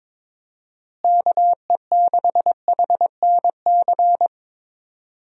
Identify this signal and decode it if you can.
Morse code, heard as a keyed tone, about 21 wpm, 700 Hz. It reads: KE6HNC
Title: USA Call Signs - 6 Characters - Repeated 2x - 22wpm